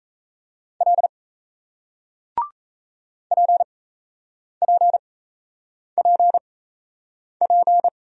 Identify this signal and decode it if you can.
Morse code, heard as a keyed tone, about 37 wpm, spaced as for 8 wpm, 700 Hz. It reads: L PPPP